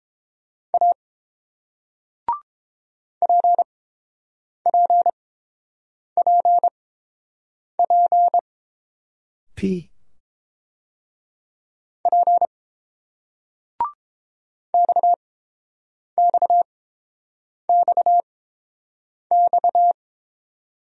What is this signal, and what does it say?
Morse code, heard as a keyed tone, about 30 wpm, spaced as for 7 wpm, 700 Hz. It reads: A PPPP P XXXX